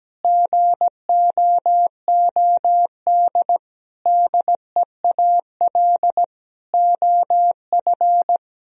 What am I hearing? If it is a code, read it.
Morse code, heard as a keyed tone, about 17 wpm, 700 Hz. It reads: GOOD DEAL OF
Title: Sentences from Top 1000 Words - Review Part 4 - 17wpm